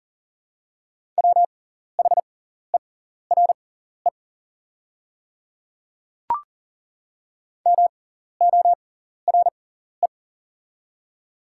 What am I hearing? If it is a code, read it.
Morse code, heard as a keyed tone, about 37 wpm, spaced as for 14 wpm, 690 Hz. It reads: WHERE MORE
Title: 6x Character Spacing - Top 300 Words - 40wpm